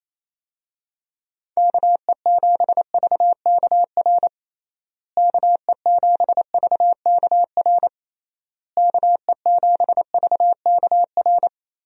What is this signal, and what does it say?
Morse code, heard as a keyed tone, about 28 wpm, 700 Hz. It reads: KE7VKR KE7VKR KE7VKR
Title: USA Call Signs - 6 Characters - Repeated 3x - 28wpm